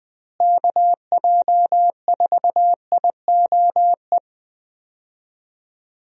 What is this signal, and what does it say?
Morse code, heard as a keyed tone, about 20 wpm, 700 Hz. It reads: KJ4IOE
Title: USA Call Signs - 6 Characters - Repeated 2x - 20wpm